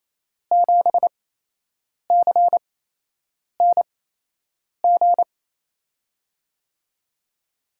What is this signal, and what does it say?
Morse code, heard as a keyed tone, about 28 wpm, spaced as for 8 wpm, 700 Hz. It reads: 7CNG
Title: USA Call Signs - 6 Characters - 8x Character Spacing - 28wpm